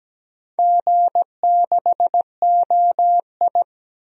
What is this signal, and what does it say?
Morse code, heard as a keyed tone, about 17 wpm, 700 Hz. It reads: G6OI
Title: DXCC-100 Call Signs - 4 Characters - 17wpm